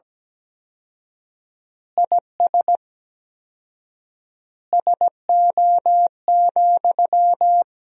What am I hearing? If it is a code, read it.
Morse code, heard as a keyed tone, about 17 wpm, 700 Hz. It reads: IS SO,
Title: 4x Word-Spacing - Sentences from Top 200 Words - 17wpm